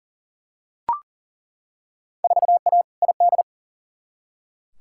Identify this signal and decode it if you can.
Morse code, heard as a keyed tone, about 39 wpm, 700 Hz. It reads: 4A ID
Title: Field Day Exchanges Encore - 40wpm